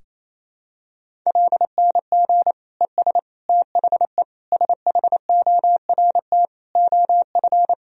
Morse code, heard as a keyed tone, about 27 wpm, 700 Hz. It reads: LNG ES THE SHORT OF
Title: Sentences from Top 400 Words and Abbreviations 28wpm